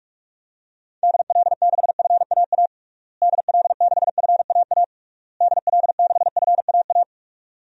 Morse code, heard as a keyed tone, about 46 wpm, 700 Hz. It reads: DL6FAA DL6FAA DL6FAA
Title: DXCC-100 Call Signs - 6 Characters - Repeated 3x - 45wpm